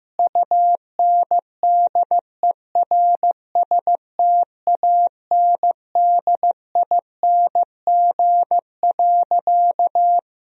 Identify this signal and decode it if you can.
Morse code, heard as a keyed tone, about 16 wpm, 700 Hz. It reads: UNDERSTANDING.
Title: Sentences from Top 1000 Words - Review Part 3 - 15wpm